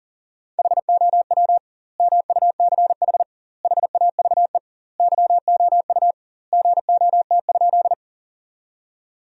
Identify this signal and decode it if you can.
Morse code, heard as a keyed tone, about 40 wpm, 700 Hz. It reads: HOW MUCH HAVE YOU GOT?
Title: 2x Word-Spacing - Sentences from Top 300 Words - 40wpm